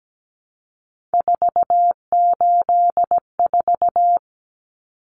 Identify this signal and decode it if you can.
Morse code, heard as a keyed tone, about 17 wpm, 700 Hz. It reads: 484
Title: Numbers - 3-Digit 17wpm